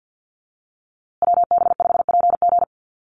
Morse code, heard as a keyed tone, about 50 wpm, 700 Hz. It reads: WB5PG
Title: USA Call Signs Repeated 2x 50wpm